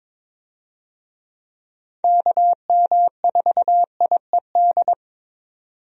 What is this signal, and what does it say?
Morse code, heard as a keyed tone, about 22 wpm, 700 Hz. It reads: KM4IED
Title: USA Call Signs - 6 Characters - 22wpm